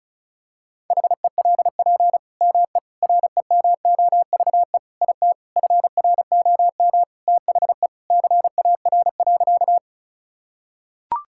Morse code, heard as a keyed tone, about 36 wpm, 700 Hz. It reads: HELP ME REMOVE IT FROM THE CAR.
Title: Sentences from Top 1000 Words - Review Part 5 - 35wpm